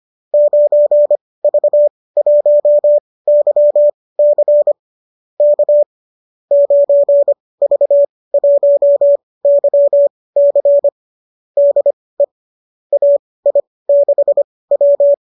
Morse code, heard as a keyed tone, about 25 wpm, spaced as for 18 wpm, 590 Hz. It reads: 9V1YC K 9V1YC DE AI6W
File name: qso207 18